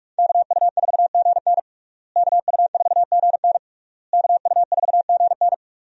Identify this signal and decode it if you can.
Morse code, heard as a keyed tone, about 41 wpm, 700 Hz. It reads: KU4GN KU4GN KU4GN
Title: USA Call Signs - 5 Characters - Repeated 3x - 45wpm